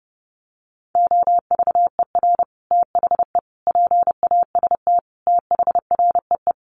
Morse code, heard as a keyed tone, about 30 wpm, 700 Hz. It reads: OVER THE PAST THREE
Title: Sets of 5 Words Encore2 30wpm